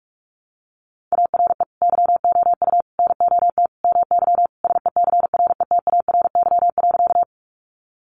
Morse code, heard as a keyed tone, about 45 wpm, 700 Hz. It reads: ARE YOU NOT MY SECRETARY.
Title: Sentences from Top 1300 Words 45wpm